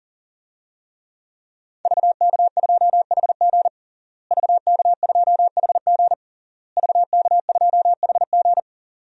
Morse code, heard as a keyed tone, about 40 wpm, 700 Hz. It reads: VK2HG VK2HG VK2HG
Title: DXCC-100 Call Signs - 5 Characters - Repeated 3x - 40wpm